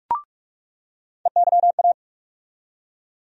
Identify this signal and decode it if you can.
Morse code, heard as a keyed tone, about 45 wpm, 700 Hz. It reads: EYA
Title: USA Call Sign Postfixes - 3 Letters 45wpm